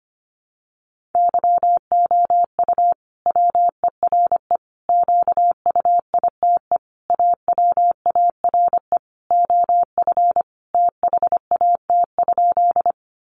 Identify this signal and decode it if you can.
Morse code, heard as a keyed tone, about 25 wpm, 700 Hz. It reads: YOU WERE QUITE AWARE OF THAT?
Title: Sentences from Top 1200 Words 25wpm